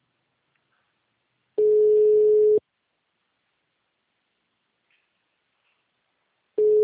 Ringback tone (Europe, 425 Hz).